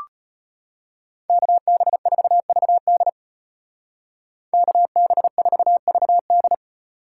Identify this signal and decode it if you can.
Morse code, heard as a keyed tone, about 31 wpm, 700 Hz. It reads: KB4VD KB4VD
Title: Speed Racing - USA 2x2 Call Signs 25wpm